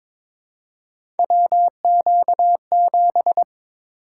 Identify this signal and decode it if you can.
Morse code, heard as a keyed tone, about 22 wpm, 700 Hz. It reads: WQ7
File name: practice / 22wpm / Three Letter-Number Combinations - 22wpm